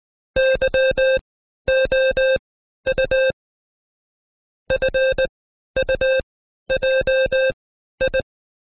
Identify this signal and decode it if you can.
Morse code, heard as a keyed tone, about 19 wpm, spaced as for 12 wpm, 530 Hz. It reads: YOU FUJI
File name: Code Quick / AudioDownld2 / Seg132-Stepper5 11WPM